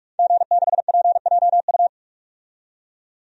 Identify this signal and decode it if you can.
Morse code, heard as a keyed tone, about 46 wpm, 700 Hz. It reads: G6PJU
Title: DXCC-100 Call Signs - 5 Characters - 45wpm